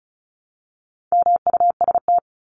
Morse code, heard as a keyed tone, about 34 wpm, 700 Hz. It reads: MUST
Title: Top 500 Words 35wpm